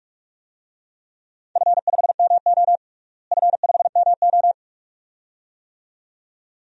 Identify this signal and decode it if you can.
Morse code, heard as a keyed tone, about 42 wpm, 700 Hz. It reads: F5MO F5MO
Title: DXCC-100 Call Signs - 4 Characters - Repeated 2x - 45wpm